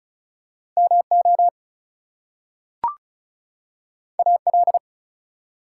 Morse code, heard as a keyed tone, about 36 wpm, 700 Hz. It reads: MO AL
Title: US States and Canadian Provinces 35wpm